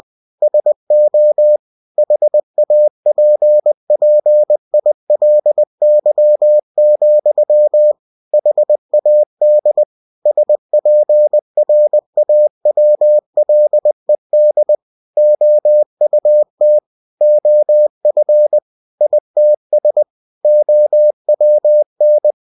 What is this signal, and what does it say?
Morse code, heard as a keyed tone, about 20 wpm, 600 Hz. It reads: SO HAPPILY, HAD SPRAWLED OUT OF ITS OWN
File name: cw-books / winnie-the-pooh / winnie-20wpm-page-0091